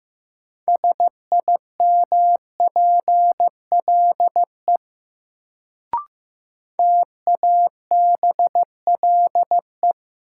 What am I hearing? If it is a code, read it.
Morse code, heard as a keyed tone, about 15 wpm, 700 Hz. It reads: SIMPLE TABLE
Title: Sentences from Top 400 Words - New Word Drill - 15wpm